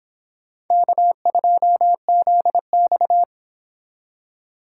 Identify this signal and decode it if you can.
Morse code, heard as a keyed tone, about 26 wpm, 700 Hz. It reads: K2ZX